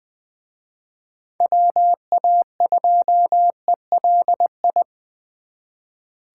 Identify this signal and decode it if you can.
Morse code, heard as a keyed tone, about 23 wpm, 700 Hz. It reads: WA2ELI
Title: Speed Racing - USA 2x3 Call Signs 20wpm